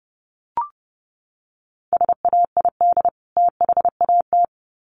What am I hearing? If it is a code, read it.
Morse code, heard as a keyed tone, about 30 wpm, 700 Hz. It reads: SAID THAT
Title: Sets of 2 Words 30wpm